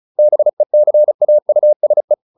Morse code, heard as a keyed tone, about 34 wpm, 600 Hz. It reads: BECAUSE